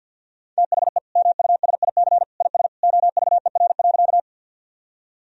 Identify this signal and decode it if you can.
Morse code, heard as a keyed tone, about 50 wpm, 700 Hz. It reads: THE MUSIC IS OVER.
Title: Sentences from Top 1000 Words - Review Part 1 - 50wpm